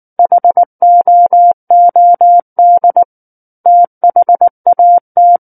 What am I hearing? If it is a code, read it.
Morse code, heard as a keyed tone, about 19 wpm, 700 Hz. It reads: HOOD THAT
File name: Book One / 10-Chapter 10 - 19 WPM